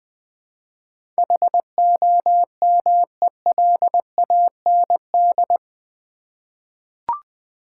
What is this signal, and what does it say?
Morse code, heard as a keyed tone, about 20 wpm, 700 Hz. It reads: HOMELAND
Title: 8 Letter Words - First 3 Letters Given - 20wpm